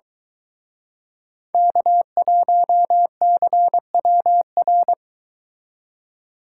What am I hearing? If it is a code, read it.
Morse code, heard as a keyed tone, about 23 wpm, 700 Hz. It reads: K1CWR